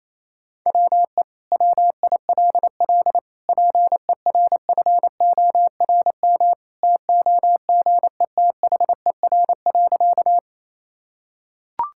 Morse code, heard as a keyed tone, about 28 wpm, 700 Hz. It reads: WE WILL PERFORM TOGETHER.
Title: Sentences from Top 1000 Words - Review Part 5 - 28wpm